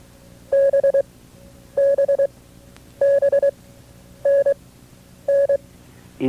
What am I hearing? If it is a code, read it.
Morse code, heard as a keyed tone, about 23 wpm, spaced as for 9 wpm, 550 Hz. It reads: BBBNN
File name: FreeFiles / CW-practice / KB6MT-tape2-side2